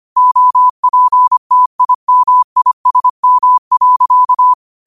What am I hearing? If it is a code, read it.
Morse code, heard as a keyed tone, about 25 wpm, 1000 Hz. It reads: OPTIMISM.
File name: qotdfiles / 25wpm